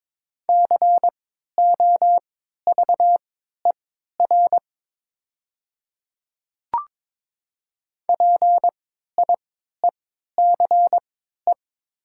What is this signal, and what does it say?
Morse code, heard as a keyed tone, about 22 wpm, spaced as for 13 wpm, 700 Hz. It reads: COVER PIECE
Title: 3x Character Spacing - Top 500 Words - 22wpm